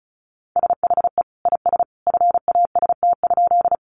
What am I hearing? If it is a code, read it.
Morse code, heard as a keyed tone, about 34 wpm, 700 Hz. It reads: SHE IS FAST?
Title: Sentences from Top 400 Words 35wpm